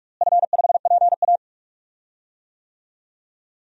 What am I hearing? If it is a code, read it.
Morse code, heard as a keyed tone, about 46 wpm, 700 Hz. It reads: F5PA
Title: DXCC-100 Call Signs - 4 Characters - Repeated 2x - 45wpm